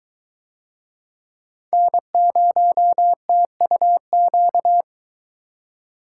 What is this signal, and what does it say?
Morse code, heard as a keyed tone, about 21 wpm, 700 Hz. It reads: N0TUQ